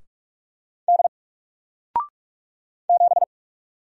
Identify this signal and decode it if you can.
Morse code, heard as a keyed tone, about 44 wpm, 700 Hz. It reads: D 7
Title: Single Letter-Number 45wpm